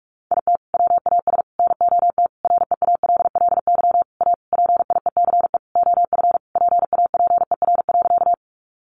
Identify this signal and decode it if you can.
Morse code, heard as a keyed tone, about 44 wpm, 700 Hz. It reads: IT WAS NOT REALLY A PIECE OF PAPER.
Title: Sentences from Top 600 Words 45wpm